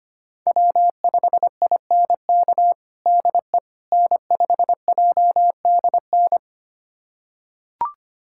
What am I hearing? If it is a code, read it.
Morse code, heard as a keyed tone, about 25 wpm, 700 Hz. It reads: W5INK DE N5JDN
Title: USA Callsign DE Callsign - 5 Characters - 25wpm